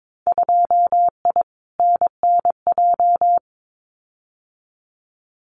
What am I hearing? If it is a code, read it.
Morse code, heard as a keyed tone, about 22 wpm, 700 Hz. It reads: 2I NNJ
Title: Winter Field Day Exchanges Encore 22wpm